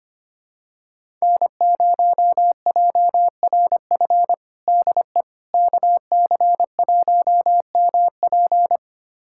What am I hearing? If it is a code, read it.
Morse code, heard as a keyed tone, about 25 wpm, 700 Hz. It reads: N0JRF DE KC1MP
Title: USA Callsign DE Callsign - 5 Characters - 25wpm